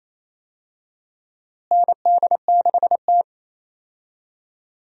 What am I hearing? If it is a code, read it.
Morse code, heard as a keyed tone, about 28 wpm, 700 Hz. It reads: ND6T
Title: USA Call Signs - 4 Characters - Repeated 2x - 28wpm